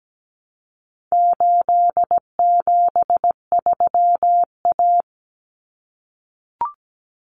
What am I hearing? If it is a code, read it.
Morse code, heard as a keyed tone, about 17 wpm, 700 Hz. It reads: 873A
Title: Common Cut Numbers - 4-Digit 17wpm